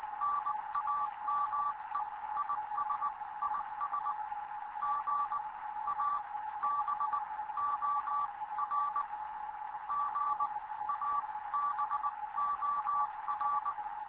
Morse code, heard as a keyed tone, about 20 wpm, spaced as for 15 wpm, 1160 Hz, with radio noise. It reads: NAME IS IS GABOR GABOR
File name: Morze-HA2MN / hangp / 2 11 GYAKORLAT